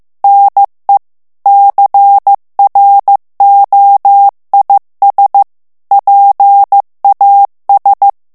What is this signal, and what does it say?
Morse code, heard as a keyed tone, about 15 wpm, 800 Hz. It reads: NE CROIS PAS
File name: morse mp3 / FRANCAIS TEXTE CLAIR 2 VITESSE 14 MOTS MINUTES